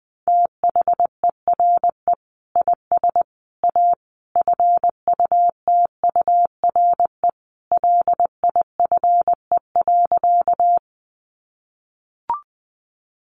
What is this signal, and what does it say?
Morse code, heard as a keyed tone, about 19 wpm, 700 Hz. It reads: THERE IS A FUTURE LIFE.
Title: Sentences from Top 700 Words 20wpm